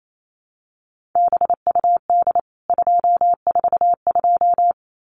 Morse code, heard as a keyed tone, about 28 wpm, 700 Hz. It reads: BUD 242
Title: CWT Exchange 28wpm